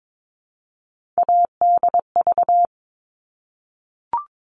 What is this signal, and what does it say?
Morse code, heard as a keyed tone, about 22 wpm, 700 Hz. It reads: ADV
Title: All Cut Numbers - 3-Digit 22wpm